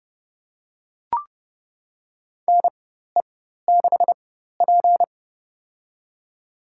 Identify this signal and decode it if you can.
Morse code, heard as a keyed tone, about 30 wpm, spaced as for 14 wpm, 700 Hz. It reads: NE6P
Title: USA Call Signs - 4 Characters - 4x Character Spacing - 30wpm